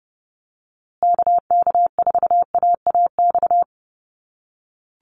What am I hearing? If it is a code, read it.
Morse code, heard as a keyed tone, about 30 wpm, 700 Hz. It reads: KK4AAX
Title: USA Call Signs Repeated 3x 30wpm